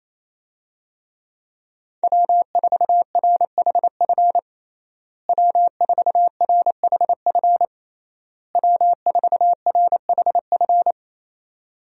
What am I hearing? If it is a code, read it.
Morse code, heard as a keyed tone, about 28 wpm, 700 Hz. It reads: W4RHF W4RHF W4RHF
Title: USA Call Signs - 5 Characters - Repeated 3x - 28wpm